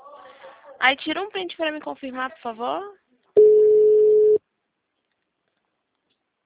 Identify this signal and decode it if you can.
Ringback tone (Europe, 425 Hz), with speech.